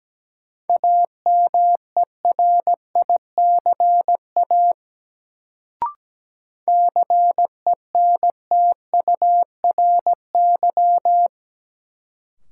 Morse code, heard as a keyed tone, about 17 wpm, 700 Hz. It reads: AMERICA CENTURY
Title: Sets of 5 Words - New Word Drill - 17wpm